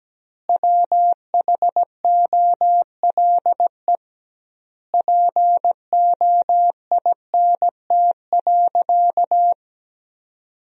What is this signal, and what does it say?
Morse code, heard as a keyed tone, about 17 wpm, 700 Hz. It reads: WHOLE POINT.
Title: 2x Word-Spacing - Sentences from Top 400 Words - 17wpm